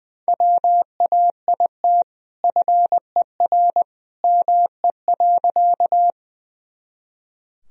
Morse code, heard as a keyed tone, about 21 wpm, 700 Hz. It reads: WAIT FER ME.
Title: Sentences from Top 500 Words and Abbreviations 20wpm